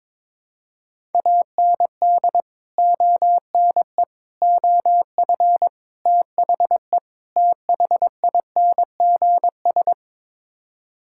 Morse code, heard as a keyed tone, about 22 wpm, 700 Hz. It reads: AND ONE OF THE THINGS
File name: practice / 22wpm / Build A Phrase - 5-Word Phrases - 22wpm